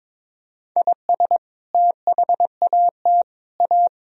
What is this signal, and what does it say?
Morse code, heard as a keyed tone, about 21 wpm, 700 Hz. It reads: IS THAT A
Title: Sentences from Top 1000 Words - Review Part 3 - 22wpm